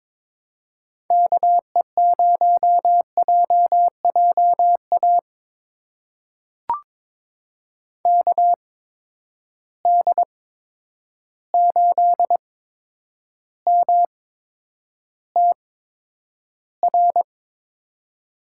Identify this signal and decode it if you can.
Morse code, heard as a keyed tone, about 22 wpm, spaced as for 9 wpm, 700 Hz. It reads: KE0JJA KD8MTR
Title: USA Call Signs - 8x Character Spacing - 22wpm